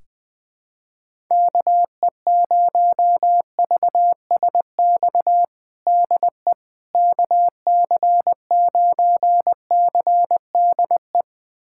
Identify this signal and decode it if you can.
Morse code, heard as a keyed tone, about 21 wpm, 700 Hz. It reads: KE0VSX DE KC9CDE
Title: USA Callsign DE Callsign 20wpm